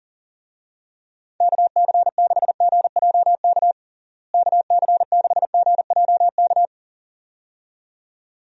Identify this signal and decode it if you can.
Morse code, heard as a keyed tone, about 37 wpm, 700 Hz. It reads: KC6GJK KC6GJK
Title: USA Call Signs - 6 Characters - Repeated 2x - 40wpm